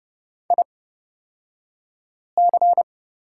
Morse code, heard as a keyed tone, about 29 wpm, spaced as for 5 wpm, 700 Hz. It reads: IC
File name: Lesson 37d - Letter x - Two Characters 30wpm